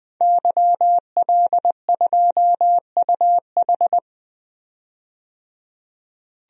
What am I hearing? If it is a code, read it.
Morse code, heard as a keyed tone, about 20 wpm, 700 Hz. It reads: YL2UH
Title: DXCC-100 Call Signs - 5 Characters - 20wpm